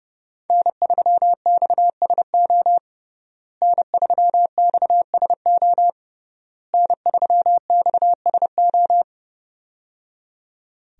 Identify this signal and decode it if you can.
Morse code, heard as a keyed tone, about 30 wpm, 700 Hz. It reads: N3XSO N3XSO N3XSO
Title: USA Call Signs - 5 Characters - Repeated 3x - 30wpm